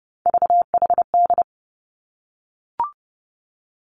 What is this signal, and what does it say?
Morse code, heard as a keyed tone, about 31 wpm, 700 Hz. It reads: VHD
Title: Three Challenging Letter-Number Combinations 30wpm